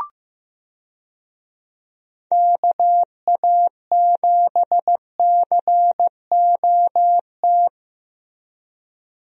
Morse code, heard as a keyed tone, about 15 wpm, 700 Hz. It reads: KA7COT
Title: USA Call Signs - 6 Characters - Repeated 2x - 15wpm